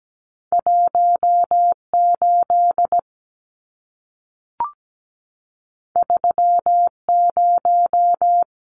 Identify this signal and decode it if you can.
Morse code, heard as a keyed tone, about 17 wpm, 700 Hz. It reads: 18 30
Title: Numbers - 2-Digit 17wpm